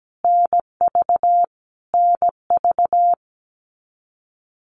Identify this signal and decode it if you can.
Morse code, heard as a keyed tone, about 17 wpm, 700 Hz. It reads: NV NV
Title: US States Repeated 3x 17wpm